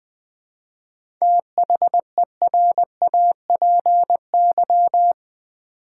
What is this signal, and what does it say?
Morse code, heard as a keyed tone, about 20 wpm, 700 Hz. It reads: THERAPY
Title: 7 Letter Words - First 3 Letters Given - 20wpm